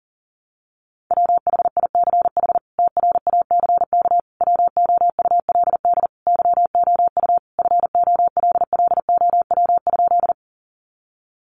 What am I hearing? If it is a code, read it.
Morse code, heard as a keyed tone, about 40 wpm, 700 Hz. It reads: WHICH TRACK WOULD YOU FOLLOW?
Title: Sentences from Top 1100 Words 40wpm